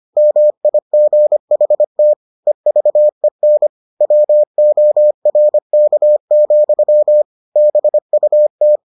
Morse code, heard as a keyed tone, about 25 wpm, 600 Hz. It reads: MIGHT EVEN WORK, BUT